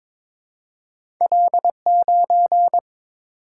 Morse code, heard as a keyed tone, about 22 wpm, 700 Hz. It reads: L9
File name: practice / 22wpm / Two Letter-Number Combinations - 22wpm